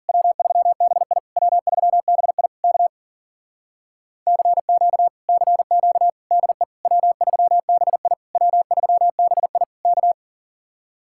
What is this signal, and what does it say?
Morse code, heard as a keyed tone, about 52 wpm, 700 Hz. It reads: W3BI W3BI K CQ CQ DE W3BI W3BI K